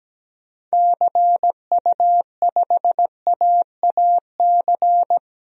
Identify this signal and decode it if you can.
Morse code, heard as a keyed tone, about 17 wpm, 700 Hz. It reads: CU5AAC
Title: DXCC-100 Call Signs - 6 Characters - 17wpm